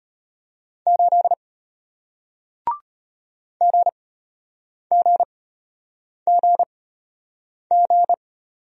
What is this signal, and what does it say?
Morse code, heard as a keyed tone, about 33 wpm, spaced as for 8 wpm, 700 Hz. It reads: 8 GGGG